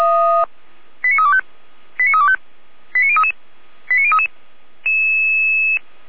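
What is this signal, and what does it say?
Five-tone selective call (ZVEI), heard as quick runs of five tones, tones of about 70 ms. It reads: 89325 89325 89033 89033
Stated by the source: ZVEI five-tone call received by SDR, from Feuerwehr Orlamünde.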